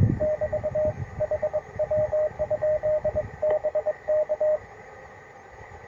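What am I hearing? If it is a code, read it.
Morse code, heard as a keyed tone, about 24 wpm, 610 Hz. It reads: <BT> HW? BK
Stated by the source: Morse on 20 m during a QRP forest field day, with broadband noise after a magnetic storm and deep, very slow fading.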